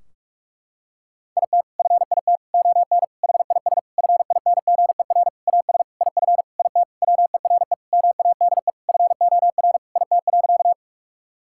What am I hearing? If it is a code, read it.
Morse code, heard as a keyed tone, about 46 wpm, 700 Hz. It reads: IT FIT ON HIS FINGER AS IF IT WERE MADE FOR IT.
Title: Sentences from Top 1600 Words 45wpm